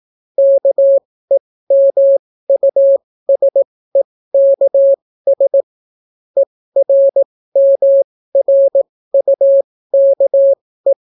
Morse code, heard as a keyed tone, about 18 wpm, spaced as for 14 wpm, 550 Hz. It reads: KEMUSEKS ERMRUKE